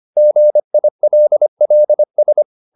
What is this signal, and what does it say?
Morse code, heard as a keyed tone, about 25 wpm, 600 Hz. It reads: GILLS